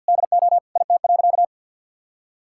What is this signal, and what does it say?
Morse code, heard as a keyed tone, about 50 wpm, 700 Hz. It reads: DO IT.
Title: Sentences from Top 1600 Words 50wpm